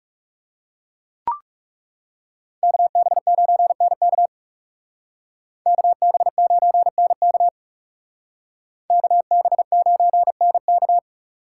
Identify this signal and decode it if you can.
Morse code, heard as a keyed tone, about 37 wpm, 700 Hz. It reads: KB9NK KB9NK KB9NK
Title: Speed Racing - USA 2x2 Call Signs 30wpm